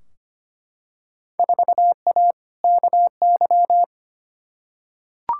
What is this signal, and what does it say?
Morse code, heard as a keyed tone, about 25 wpm, 700 Hz. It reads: 4A KY
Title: Field Day Exchanges Encore - 25wpm